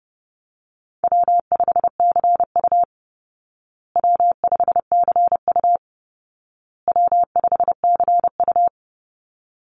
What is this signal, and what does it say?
Morse code, heard as a keyed tone, about 28 wpm, 700 Hz. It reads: W5CU W5CU W5CU
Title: Easy Call Signs Repeated 3x - 30wpm